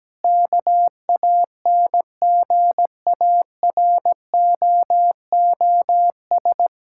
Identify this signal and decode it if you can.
Morse code, heard as a keyed tone, about 17 wpm, 700 Hz. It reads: KANGAROOS
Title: Pangram Challenge v2 - 17wpm